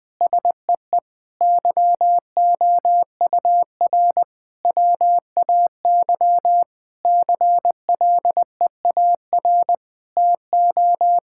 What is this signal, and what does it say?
Morse code, heard as a keyed tone, about 20 wpm, 700 Hz. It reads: SEE YOUR WAY CLEAR TO
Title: Sentences with 8-Word Phrases - Spoken Beginning - 20wpm